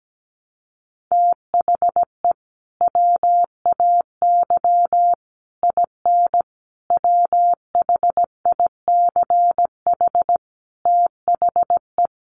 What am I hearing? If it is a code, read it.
Morse code, heard as a keyed tone, about 17 wpm, 700 Hz. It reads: THE WAY IN WHICH THE
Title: Sets of 5 Words Encore2 17wpm